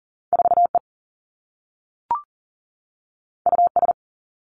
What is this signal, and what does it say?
Morse code, heard as a keyed tone, about 40 wpm, 700 Hz. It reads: 4 E U S
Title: Two Challenging Letter-Number Combinations 40wpm